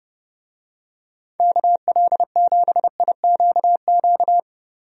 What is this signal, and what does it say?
Morse code, heard as a keyed tone, about 30 wpm, 700 Hz. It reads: KL7IQQ
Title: DXCC-100 Call Signs - 6 Characters - 30wpm